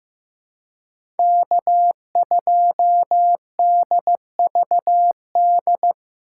Morse code, heard as a keyed tone, about 15 wpm, 700 Hz. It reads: K2DVD
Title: USA Call Signs - 5 Characters - 2x Character Spacing - 15wpm